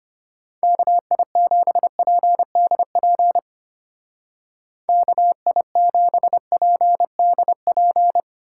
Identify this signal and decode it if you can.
Morse code, heard as a keyed tone, about 30 wpm, 700 Hz. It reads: KI7PDP KI7PDP
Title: Speed Racing - USA 2x3 Call Signs 25wpm